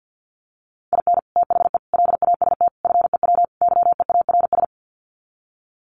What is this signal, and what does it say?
Morse code, heard as a keyed tone, about 50 wpm, 700 Hz. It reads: IN THE LAST FEW YEARS
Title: Sets of 5 Words 50wpm